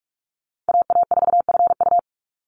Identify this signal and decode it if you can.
Morse code, heard as a keyed tone, about 44 wpm, 700 Hz. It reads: AA4FU
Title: Call Signs Repeated 2x - 45wpm